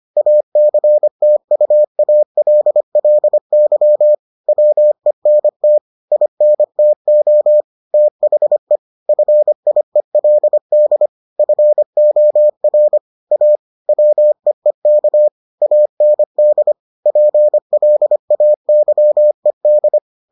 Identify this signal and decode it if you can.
Morse code, heard as a keyed tone, about 25 wpm, 600 Hz. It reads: ACTUALLY WENT INTO THE FIELD FOR A WEEK AND PLAYED